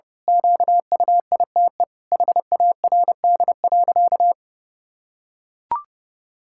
Morse code, heard as a keyed tone, about 30 wpm, 700 Hz. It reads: QUITE HARD.
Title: Pangram Challenge v2 - 30wpm